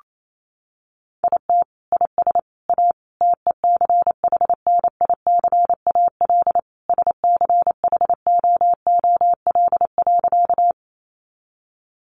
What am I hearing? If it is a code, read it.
Morse code, heard as a keyed tone, about 28 wpm, 700 Hz. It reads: IT IS A TECHNICAL SCHOOL.
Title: Sentences from Top 1100 Words 28wpm